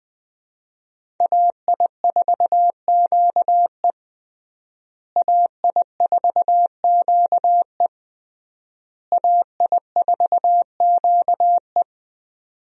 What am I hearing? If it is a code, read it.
Morse code, heard as a keyed tone, about 20 wpm, 700 Hz. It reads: AI4QE AI4QE AI4QE
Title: USA Call Signs - 5 Characters - Repeated 3x - 20wpm